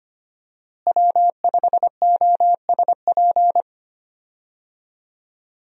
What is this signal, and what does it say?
Morse code, heard as a keyed tone, about 25 wpm, 700 Hz. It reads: W5OSP
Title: USA Call Signs - 5 Characters - Repeated 3x - 25wpm